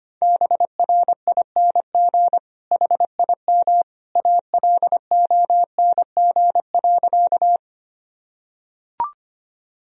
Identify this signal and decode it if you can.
Morse code, heard as a keyed tone, about 25 wpm, 700 Hz. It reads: BRING HIM ALONG.
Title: Sentences from Top 1000 Words - Review Part 3 - 25wpm